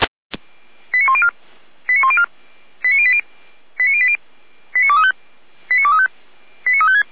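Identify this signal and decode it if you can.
Five-tone selective call (ZVEI), heard as quick runs of five tones, tones of about 70 ms. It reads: 89104 89104 89080 89080 89235 89235 89356
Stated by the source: ZVEI five-tone selective call 89280, received by SDR.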